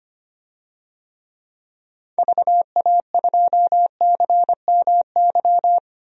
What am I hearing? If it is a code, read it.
Morse code, heard as a keyed tone, about 25 wpm, 700 Hz. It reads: VA2CMY
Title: DXCC-100 Call Signs - 6 Characters - 25wpm